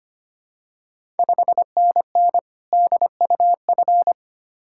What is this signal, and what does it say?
Morse code, heard as a keyed tone, about 25 wpm, 700 Hz. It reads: 5NN DUF
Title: Ontario QSO Party Exchange 25wpm